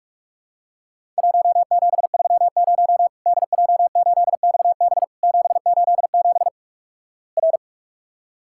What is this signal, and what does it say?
Morse code, heard as a keyed tone, about 46 wpm, 700 Hz. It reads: 1730 DJ8XB 787 R
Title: Worked All Europe DX Contest - QTC Exchange - 45wpm